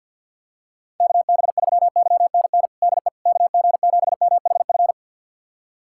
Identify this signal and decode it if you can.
Morse code, heard as a keyed tone, about 51 wpm, 700 Hz. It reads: KB3YNN DE KG7MHF
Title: USA Callsign DE Callsign - 6 Characters - 50wpm